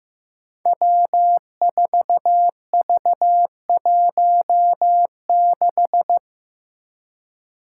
Morse code, heard as a keyed tone, about 15 wpm, 700 Hz. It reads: W4V16
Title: Five Challenging Letter-Number Combinations 15wpm